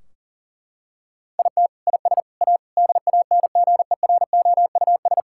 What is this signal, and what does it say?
Morse code, heard as a keyed tone, about 40 wpm, 700 Hz. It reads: IT IS A DANGEROUS
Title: Sentences from Top 1600 Words 40wpm